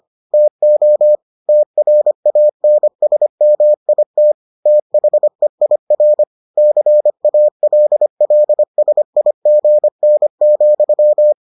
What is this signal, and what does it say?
Morse code, heard as a keyed tone, about 25 wpm, 600 Hz. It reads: TO TRANSMIT THEIR CALLSIGN,